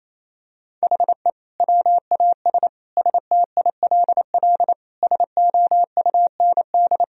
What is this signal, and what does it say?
Morse code, heard as a keyed tone, about 28 wpm, 700 Hz. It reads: HE WAS STILL SOUND
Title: Sentences from Top 1000 Words - Review Part 2 - 28wpm